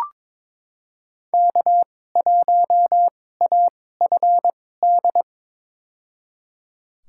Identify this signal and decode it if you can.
Morse code, heard as a keyed tone, about 22 wpm, spaced as for 16 wpm, 700 Hz. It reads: K1AFD
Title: USA Call Signs - 5 Characters - 2x Character Spacing - 22wpm